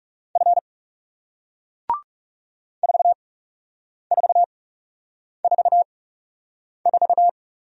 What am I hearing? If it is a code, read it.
Morse code, heard as a keyed tone, about 39 wpm, spaced as for 8 wpm, 700 Hz. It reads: F 4444